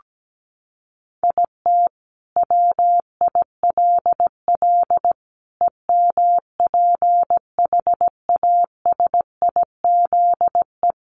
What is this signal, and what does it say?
Morse code, heard as a keyed tone, about 17 wpm, 700 Hz. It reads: IT WILL EMPHASIZE
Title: Sentences from Top 1500 Words 17wpm